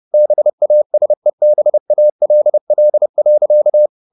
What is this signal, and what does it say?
Morse code, heard as a keyed tone, about 30 wpm, 600 Hz. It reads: BASEBALL.
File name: the old man and the sea / the old man and the sea 1 30wpm